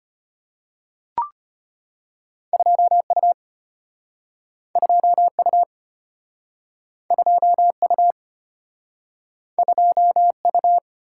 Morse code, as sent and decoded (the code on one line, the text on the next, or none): ..--- ..- / ..--- ..- / ..--- ..- / ..--- ..-
2U 2U 2U 2U